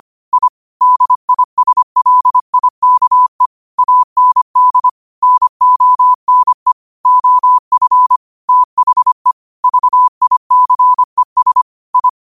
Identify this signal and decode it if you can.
Morse code, heard as a keyed tone, about 25 wpm, 1000 Hz. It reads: I DISLIKE AND NONE OF THE VICES I
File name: qotdfiles / 25wpm